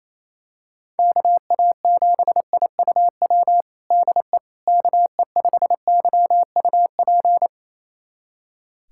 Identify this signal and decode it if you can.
Morse code, heard as a keyed tone, about 28 wpm, 700 Hz. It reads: KA7IUW DE KE5YUP
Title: USA Callsign DE Callsign - 6 Characters - 28wpm